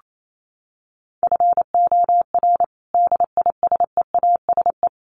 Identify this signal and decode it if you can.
Morse code, heard as a keyed tone, about 28 wpm, 700 Hz. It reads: FOR DISEASE
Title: Sets of 5 Words Encore2 28wpm